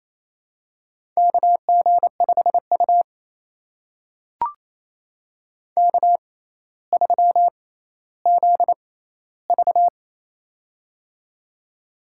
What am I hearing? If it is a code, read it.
Morse code, heard as a keyed tone, about 28 wpm, spaced as for 14 wpm, 700 Hz. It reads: KG5U K3ZV
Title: USA Call Signs - 6x Character Spacing - 28wpm